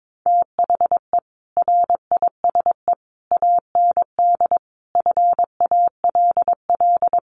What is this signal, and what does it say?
Morse code, heard as a keyed tone, about 22 wpm, 700 Hz. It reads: THE RISE AND FALL
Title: Sets of 5 Words Encore2 22wpm